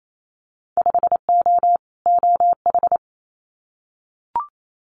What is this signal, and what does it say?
Morse code, heard as a keyed tone, about 28 wpm, 700 Hz. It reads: 5O OH
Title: Winter Field Day Exchanges Encore 28wpm